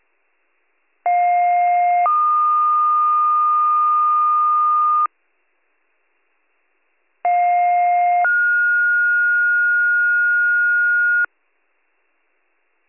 Two-tone page, heard as a one-second tone, then a three-second tone, about 710 then 1220 Hz, about 710 then 1430 Hz. It is heard from 1.1 to 11.3 s.